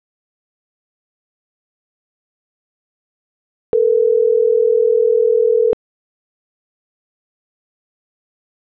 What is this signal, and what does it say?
Ringback tone (North America), 440 + 480 Hz.